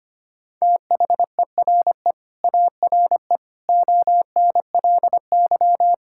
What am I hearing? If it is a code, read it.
Morse code, heard as a keyed tone, about 25 wpm, 700 Hz. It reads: THERE ARE ONLY
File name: practice / 25wpm / Sentences with 9-Word Phrases - 25wpm